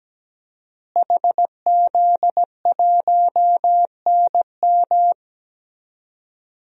Morse code, heard as a keyed tone, about 17 wpm, 700 Hz. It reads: HZ1NM
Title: DXCC-100 Call Signs - 5 Characters - Repeated 2x - 17wpm